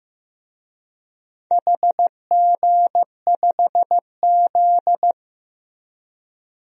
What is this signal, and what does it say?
Morse code, heard as a keyed tone, about 15 wpm, 700 Hz. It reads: HG5Z
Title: DXCC-100 Call Signs - 4 Characters - Repeated 2x - 15wpm